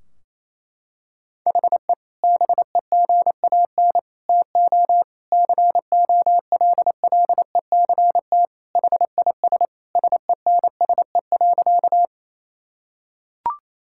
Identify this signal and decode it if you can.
Morse code, heard as a keyed tone, about 28 wpm, 700 Hz. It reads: HE BEGAN TO COLLECT HIS SENSE.
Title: Sentences from Top 1600 Words 28wpm